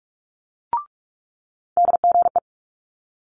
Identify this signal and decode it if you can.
Morse code, heard as a keyed tone, about 46 wpm, 700 Hz. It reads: DGE
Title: Multi-letter Phonemes Quick Fire 45wpm